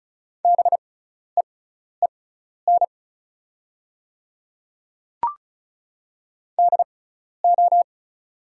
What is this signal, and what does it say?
Morse code, heard as a keyed tone, about 36 wpm, spaced as for 12 wpm, 700 Hz. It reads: BEEN DO
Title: 6x Character Spacing - Top 100 Words - 35wpm